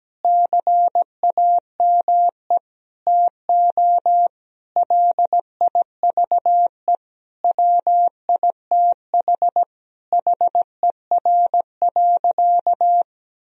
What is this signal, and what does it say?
Morse code, heard as a keyed tone, about 17 wpm, 700 Hz. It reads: CAME TO LIVE WITH HER.
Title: Sentences from Top 1000 Words - Review Part 3 - 17wpm